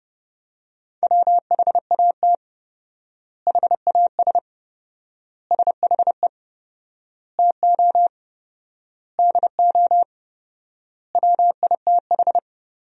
Morse code, heard as a keyed tone, about 30 wpm, 700 Hz. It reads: WHAT HAS SHE TO DO WITH
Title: 4x Word-Spacing - Sentences from Top 100 Words - 30wpm